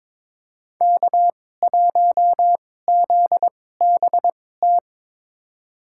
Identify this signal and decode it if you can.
Morse code, heard as a keyed tone, about 21 wpm, spaced as for 16 wpm, 700 Hz. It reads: K1ZBT
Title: USA Call Signs - 2x Character Spacing - 22wpm